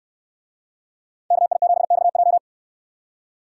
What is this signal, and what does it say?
Morse code, heard as a keyed tone, about 69 wpm, 700 Hz. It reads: KI7XJ